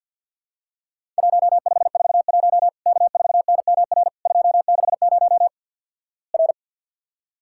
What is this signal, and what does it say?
Morse code, heard as a keyed tone, about 50 wpm, 700 Hz. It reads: 1541 K4NMR 260 R
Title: Worked All Europe DX Contest - QTC Exchange with Repeats - 50wpm